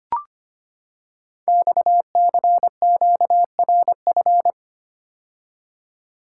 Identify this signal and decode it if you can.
Morse code, heard as a keyed tone, about 25 wpm, 700 Hz. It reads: XCQRF
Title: Code Groups - 5 Letters - 25wpm